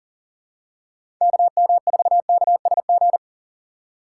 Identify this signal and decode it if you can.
Morse code, heard as a keyed tone, about 40 wpm, 700 Hz. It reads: KM4KSG
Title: USA Call Signs - 6 Characters - 3x Character Spacing - 40wpm